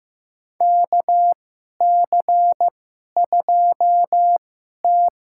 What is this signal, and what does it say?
Morse code, heard as a keyed tone, about 15 wpm, spaced as for 11 wpm, 700 Hz. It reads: KC2T
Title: USA Call Signs - 2x Character Spacing - 15wpm